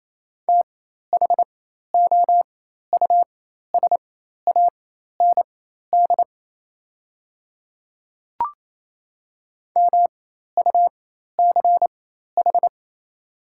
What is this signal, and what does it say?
Morse code, heard as a keyed tone, about 28 wpm, spaced as for 13 wpm, 690 Hz. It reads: THOUSAND MUCH
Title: 4x Character Spacing - Top 500 Words - 28wpm